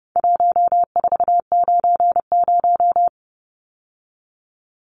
Morse code, heard as a keyed tone, about 30 wpm, 700 Hz. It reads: 1490
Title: Numbers - 4-Digit 30wpm